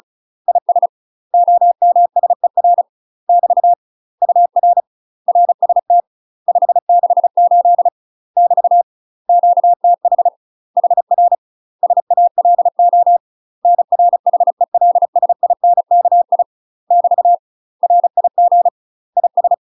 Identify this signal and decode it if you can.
Morse code, heard as a keyed tone, about 36 wpm, 700 Hz. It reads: IS OMSER <BT> UR RST 568 <BT> QTH HR SALO NR HELSINKI <BT> RIG IS
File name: UK MASTER QSOs TRAINING / 35WPM / UK MASTER QSOs 0014